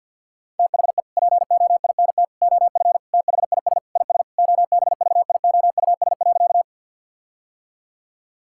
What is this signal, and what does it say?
Morse code, heard as a keyed tone, about 50 wpm, 700 Hz. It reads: THE POINT OF THIS IS OBVIOUS.
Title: Sentences from Top 1600 Words 50wpm